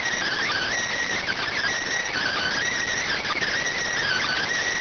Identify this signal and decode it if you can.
Emergency Alert System (SAME) tones (header bursts), heard as harsh buzzing bursts. It is heard from 0.2 to 4.4 s.